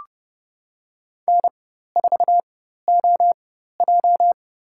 Morse code, heard as a keyed tone, about 31 wpm, spaced as for 14 wpm, 700 Hz. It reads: N4OJ
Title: USA Call Signs - 4 Characters - 4x Character Spacing - 30wpm